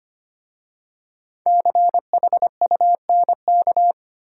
Morse code, heard as a keyed tone, about 25 wpm, 700 Hz. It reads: CHUNK
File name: 5 Letter Words - First 3 Letters Given - 25wpm